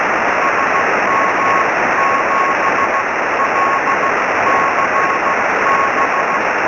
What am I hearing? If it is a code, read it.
Morse code, heard as a keyed tone, about 15 wpm, spaced as for 13 wpm, 1060 Hz, with noise. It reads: SM7LCB
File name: sm7lcb / gm3sbc 23cm